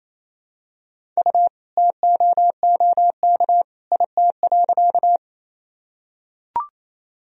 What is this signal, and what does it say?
Morse code, heard as a keyed tone, about 28 wpm, 700 Hz. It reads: U TOOK IT.
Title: Sentences from Top 300 Words and Abbreviations 28wpm